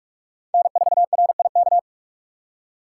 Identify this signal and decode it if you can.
Morse code, heard as a keyed tone, about 46 wpm, 700 Hz. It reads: N4RIK
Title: USA Call Signs - 5 Characters - 2x Character Spacing - 45wpm